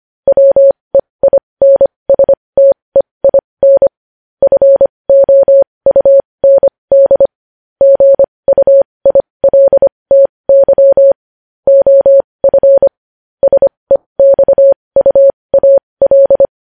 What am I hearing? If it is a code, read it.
Morse code, heard as a keyed tone, about 25 wpm, spaced as for 20 wpm, 550 Hz. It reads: WEINSTEIN FOUND GUILTY OF SEXUAL
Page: News Headlines in Morse Code at 20 WPM